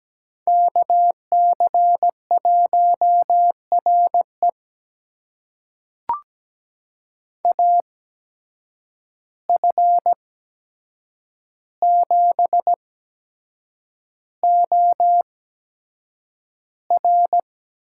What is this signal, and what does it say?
Morse code, heard as a keyed tone, about 17 wpm, spaced as for 7 wpm, 700 Hz. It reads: KC1RE AF7OR
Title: USA Call Signs - 5 Characters - 8x Character Spacing - 17wpm